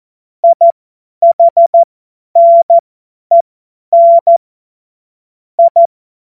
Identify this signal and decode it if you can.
Morse code, heard as a keyed tone, about 14 wpm, spaced as for 10 wpm, 680 Hz. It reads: IHNEN I
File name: Morsekurs / Witze / Witz 01. August 2023 50 F70